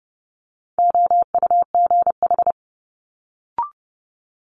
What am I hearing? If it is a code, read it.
Morse code, heard as a keyed tone, about 32 wpm, 700 Hz. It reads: OUGH
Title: Multi-letter Phonemes Quick Fire 30wpm